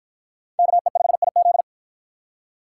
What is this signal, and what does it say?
Morse code, heard as a keyed tone, about 53 wpm, 700 Hz. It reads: KE5IZ